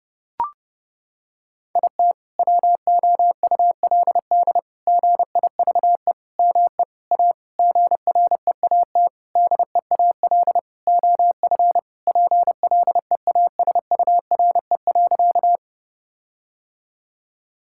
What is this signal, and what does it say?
Morse code, heard as a keyed tone, about 30 wpm, 700 Hz. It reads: IT WOULD GIVE ME A GREAT DEAL OF PLEASURE.
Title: Sentences from Top 1600 Words 30wpm